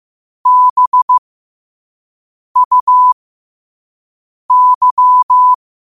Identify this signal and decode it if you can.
Morse code, heard as a keyed tone, about 15 wpm, spaced as for 5 wpm, 1000 Hz. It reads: BUY
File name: qotdfiles / 5wpm